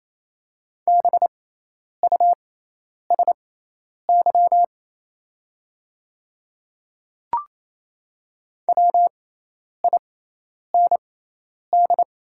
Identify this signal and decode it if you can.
Morse code, heard as a keyed tone, about 28 wpm, spaced as for 10 wpm, 700 Hz. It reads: BUSY WIND
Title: 6x Character Spacing - Top 500 Words - 28wpm